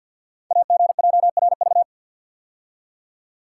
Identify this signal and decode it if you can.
Morse code, heard as a keyed tone, about 52 wpm, 690 Hz. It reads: AGJRV
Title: Code Groups - 5 Letter:Numbers - 50wpm